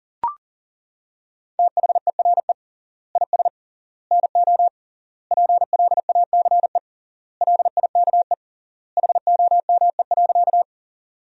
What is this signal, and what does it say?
Morse code, heard as a keyed tone, about 37 wpm, 700 Hz. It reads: THERE IS NO PLACE LIKE HOME.
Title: 3x Word-Spacing - Sentences from Top 200 Words - 40wpm